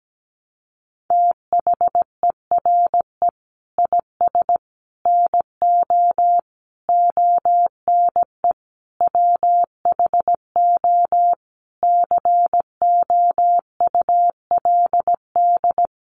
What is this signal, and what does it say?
Morse code, heard as a keyed tone, about 17 wpm, 700 Hz. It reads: THERE IS NO ONE WHO COULD
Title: Sentences from Top 1100 Words 17wpm